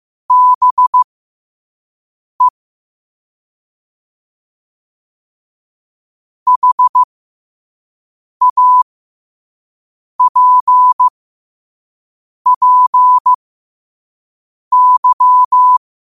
Morse code, heard as a keyed tone, about 15 wpm, spaced as for 5 wpm, 1000 Hz. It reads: BE HAPPY
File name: qotdfiles / 5wpm